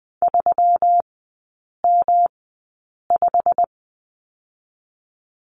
Morse code, heard as a keyed tone, about 20 wpm, 700 Hz. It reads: 3 M 5
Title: 127 Lesson 21e - Number 6 - Three Characters 20wpm